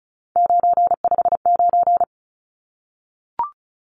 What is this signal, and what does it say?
Morse code, heard as a keyed tone, about 34 wpm, 700 Hz. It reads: 959
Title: Numbers - 3-Digit 35wpm